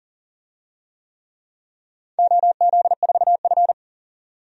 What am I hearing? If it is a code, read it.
Morse code, heard as a keyed tone, about 38 wpm, 700 Hz. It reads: OZ4F
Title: DXCC-100 Call Signs - 4 Characters - 40wpm